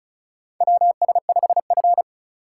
Morse code, heard as a keyed tone, about 36 wpm, 700 Hz. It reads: WS5F
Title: USA Call Signs - 4 Characters - 3x Character Spacing - 35wpm